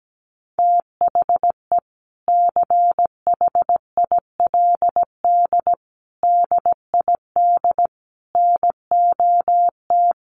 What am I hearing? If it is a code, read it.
Morse code, heard as a keyed tone, about 17 wpm, 700 Hz. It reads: THE CHILD DID NOT
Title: Sentences from Top 800 Words 17wpm